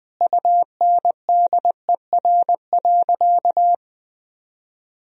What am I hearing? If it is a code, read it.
Morse code, heard as a keyed tone, about 21 wpm, 700 Hz. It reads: UNDER.
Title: Sentences from Top 1000 Words - Review Part 1 - 20wpm